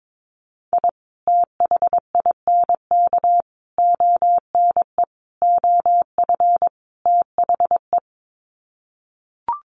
Morse code, heard as a keyed tone, about 22 wpm, 700 Hz. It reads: I THINK ONE OF THE
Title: Sets of 5 Words 22wpm